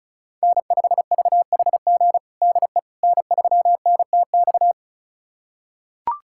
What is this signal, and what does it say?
Morse code, heard as a keyed tone, about 36 wpm, 700 Hz. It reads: N5VHG DE N3NTX
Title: USA Callsign DE Callsign - 5 Characters - 35wpm